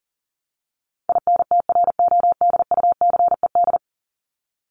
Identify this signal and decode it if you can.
Morse code, heard as a keyed tone, about 40 wpm, 700 Hz. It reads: INTRODUCED
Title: ED Suffix - Top 100 Words with Base Words 40wpm